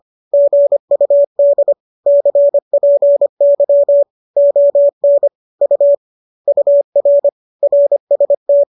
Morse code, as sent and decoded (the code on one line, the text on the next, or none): --. ..- -.. / -.-. .--. -.-- / --- -. / ..- / ..- .-. / .-. ... -
GUD CPY ON U UR RST